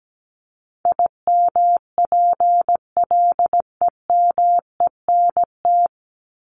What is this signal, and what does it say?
Morse code, heard as a keyed tone, about 17 wpm, 700 Hz. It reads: IMPLEMENT
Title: Top 1501-1750 Words 17wpm